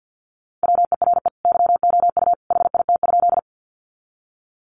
Morse code, heard as a keyed tone, about 50 wpm, 700 Hz. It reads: WERE YOU HIT?
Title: Sentences from Top 1200 Words 50wpm